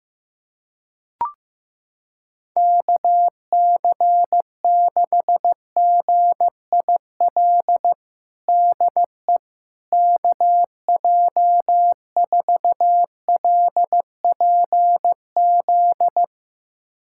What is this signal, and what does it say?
Morse code, heard as a keyed tone, about 16 wpm, 700 Hz. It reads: KC6GIL DE KJ4LPZ
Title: USA Callsign DE Callsign - 6 Characters - 15wpm